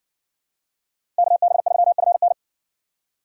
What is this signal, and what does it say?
Morse code, heard as a keyed tone, about 56 wpm, 700 Hz. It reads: KB3FN